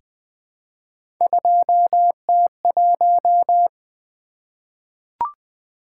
Morse code, heard as a keyed tone, about 20 wpm, 700 Hz. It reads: 2T1
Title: Three Letter-Number Combinations - 20wpm